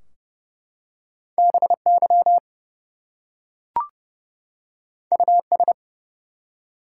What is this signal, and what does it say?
Morse code, heard as a keyed tone, about 30 wpm, 700 Hz. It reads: BY US
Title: Two Letter Words - 30wpm